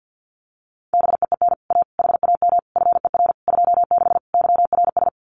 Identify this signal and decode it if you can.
Morse code, heard as a keyed tone, about 50 wpm, 700 Hz. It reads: BEEN A HAM FER 26 YRS
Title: Common QSO Phrases Encore 50wpm